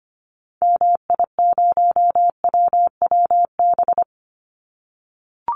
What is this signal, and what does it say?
Morse code, heard as a keyed tone, about 25 wpm, 700 Hz. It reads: MI0WWB
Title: Call Signs - 25wpm